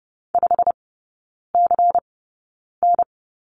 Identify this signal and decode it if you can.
Morse code, heard as a keyed tone, about 30 wpm, spaced as for 9 wpm, 700 Hz. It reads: 5CN
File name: Lesson 16e - Letter c - Three Characters 30wpm